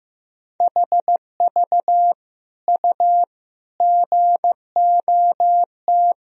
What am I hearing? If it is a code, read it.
Morse code, heard as a keyed tone, about 16 wpm, 700 Hz. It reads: HV U GOT
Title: Sentences from Top 300 Words and Abbreviations 15wpm